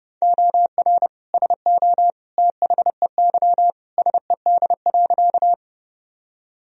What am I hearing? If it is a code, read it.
Morse code, heard as a keyed tone, about 30 wpm, 700 Hz. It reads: OR SO THEY SED.
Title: Sentences from Top 100 Words and Abbreviations 30wpm